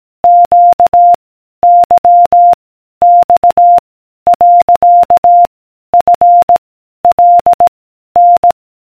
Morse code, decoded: Q Y X . F L N